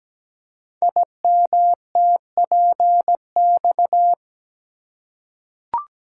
Morse code, heard as a keyed tone, about 17 wpm, 700 Hz. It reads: IMTPX
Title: Code Groups - 5 Letters - 17wpm